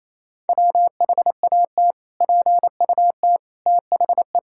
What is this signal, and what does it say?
Morse code, heard as a keyed tone, about 28 wpm, 700 Hz. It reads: WHAT PUT THE
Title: Sentences from Top 1000 Words - Review Part 5 - 28wpm